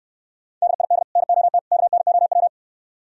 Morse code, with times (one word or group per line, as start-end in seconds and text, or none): 0.6-1.0 DID
1.2-1.6 NOT
1.7-2.5 KNOW